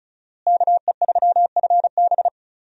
Morse code, heard as a keyed tone, about 34 wpm, 700 Hz. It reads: KE3FB
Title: USA Call Signs - 5 Characters - Repeated 3x - 35wpm